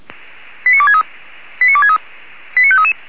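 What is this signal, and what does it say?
Five-tone selective call (ZVEI), heard as quick runs of five tones, tones of about 80 ms. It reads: 89263 89263 89533
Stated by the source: ZVEI five-tone call 89461 received by SDR on channel 464.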